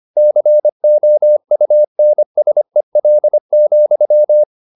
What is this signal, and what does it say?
Morse code, heard as a keyed tone, about 25 wpm, 600 Hz. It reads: COUNSEL,